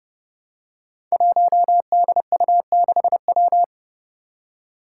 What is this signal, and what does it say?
Morse code, heard as a keyed tone, about 31 wpm, 700 Hz. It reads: 1DU6W
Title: Five Challenging Letter-Number Combinations 30wpm